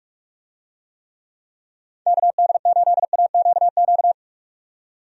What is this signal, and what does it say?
Morse code, heard as a keyed tone, about 46 wpm, 700 Hz. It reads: KD8AQQ
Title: USA Call Signs - 6 Characters - 45wpm